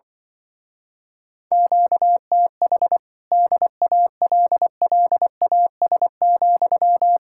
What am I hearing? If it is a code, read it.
Morse code, heard as a keyed tone, about 24 wpm, 700 Hz. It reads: QTH DALLAS,
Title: Speed-Racing Common QSO Phrases - 20wpm